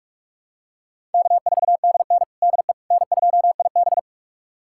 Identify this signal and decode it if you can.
Morse code, heard as a keyed tone, about 46 wpm, 700 Hz. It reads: K4DN DE N2IB
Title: USA Callsign DE Callsign - 4 Characters - 45wpm